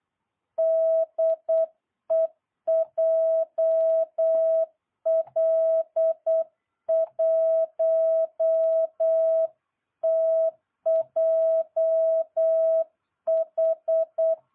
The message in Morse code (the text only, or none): DE JL1TJH